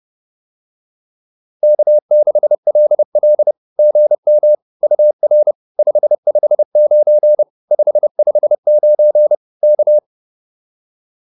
Morse code, heard as a keyed tone, about 30 wpm, 600 Hz. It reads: K6LL GM UR 559 559 K
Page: advanced-practice-files